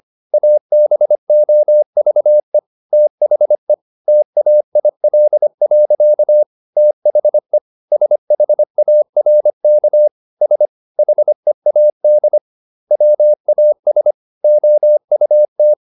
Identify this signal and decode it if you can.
Morse code, heard as a keyed tone, about 25 wpm, 600 Hz. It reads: ABOVE THE TAIL. THE SHARK S HEAD WAS OUT